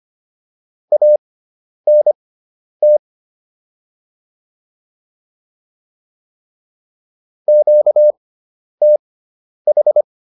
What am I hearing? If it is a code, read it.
Morse code, heard as a keyed tone, about 25 wpm, spaced as for 10 wpm, 600 Hz. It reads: ANT QTH